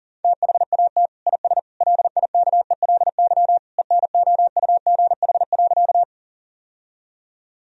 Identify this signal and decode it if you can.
Morse code, heard as a keyed tone, about 37 wpm, 700 Hz. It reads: THAT IS LIKELY ENOUGH.
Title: Sentences from Top 1000 Words - Review Part 4 - 40wpm